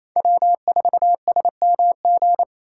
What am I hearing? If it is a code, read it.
Morse code, heard as a keyed tone, about 28 wpm, 700 Hz. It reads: W4SMG
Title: USA Call Signs - 5 Characters - 4x Character Spacing - 28wpm